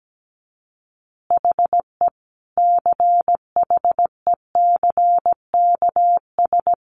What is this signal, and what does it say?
Morse code, heard as a keyed tone, about 17 wpm, 700 Hz. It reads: HE CHECKS
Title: Easy Phrases 17wpm